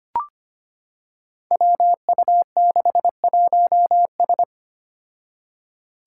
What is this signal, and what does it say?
Morse code, heard as a keyed tone, about 25 wpm, 700 Hz. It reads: WU61S
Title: Five Challenging Letter-Number Combinations 25wpm